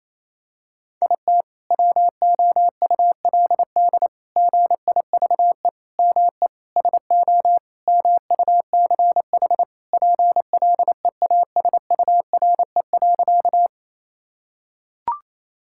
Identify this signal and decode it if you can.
Morse code, heard as a keyed tone, about 28 wpm, 700 Hz. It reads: IT WOULD GIVE ME SO MUCH PLEASURE.
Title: Sentences from Top 1600 Words 28wpm